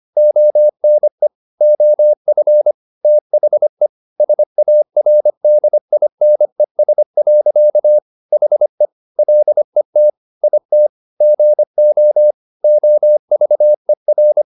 Morse code, heard as a keyed tone, about 25 wpm, 600 Hz. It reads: ONE OF THE SARDINES. HE LET IT GO OVER
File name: the old man and the sea part 1 25wpm